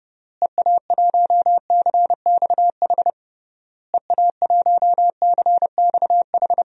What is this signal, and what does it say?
Morse code, heard as a keyed tone, about 32 wpm, 700 Hz. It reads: EA1CXH EA1CXH
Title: DXCC-100 Call Signs - 6 Characters - Repeated 3x - 30wpm